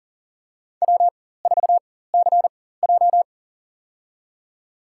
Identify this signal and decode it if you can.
Morse code, heard as a keyed tone, about 40 wpm, spaced as for 19 wpm, 700 Hz. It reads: W4CJ
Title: USA Call Signs - 4 Characters - 4x Character Spacing - 40wpm